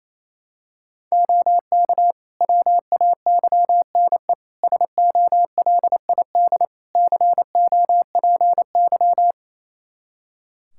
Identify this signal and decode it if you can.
Morse code, heard as a keyed tone, about 28 wpm, 700 Hz. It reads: OK WAYNE SOLID COPY
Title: Practice QSO - Exchange 3 - More Info - 28wpm